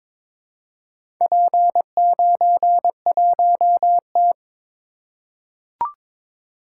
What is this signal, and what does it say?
Morse code, heard as a keyed tone, about 22 wpm, 700 Hz. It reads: P91T
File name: practice / 22wpm / Four Letter-Number Combinations - 22wpm